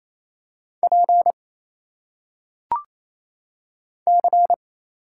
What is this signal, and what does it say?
Morse code, heard as a keyed tone, about 27 wpm, 700 Hz. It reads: P C
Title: Single Letters - Phonetic Pronunciation 28wpm